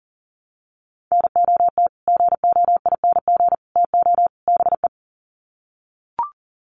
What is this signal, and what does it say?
Morse code, heard as a keyed tone, about 40 wpm, 700 Hz. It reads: NOT GOING TO BE
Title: Sets of 4 Words 40wpm